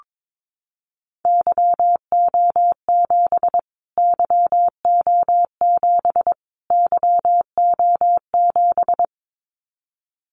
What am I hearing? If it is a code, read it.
Morse code, heard as a keyed tone, about 22 wpm, 700 Hz. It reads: YO7 YO7 YO7
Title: Callsign Prefixes for Top 25 DXCC - Repeated 3x - 22wpm